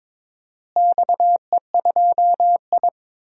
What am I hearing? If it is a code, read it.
Morse code, heard as a keyed tone, about 22 wpm, 700 Hz. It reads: XE2I
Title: DXCC-100 Call Signs - 4 Characters - Repeated 3x - 22wpm